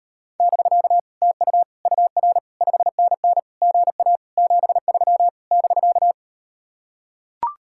Morse code, heard as a keyed tone, about 39 wpm, 700 Hz. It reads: <BK> TU UR 5NN GA 73 <BK>